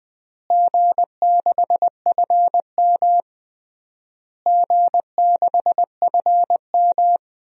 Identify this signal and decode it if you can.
Morse code, heard as a keyed tone, about 20 wpm, 700 Hz. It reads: G6FM G6FM
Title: DXCC-100 Call Signs - 4 Characters - Repeated 3x - 20wpm